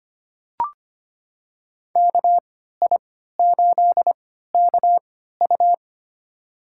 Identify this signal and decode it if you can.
Morse code, heard as a keyed tone, about 25 wpm, spaced as for 14 wpm, 700 Hz. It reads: KI8KU
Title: USA Call Signs - 5 Characters - 3x Character Spacing - 25wpm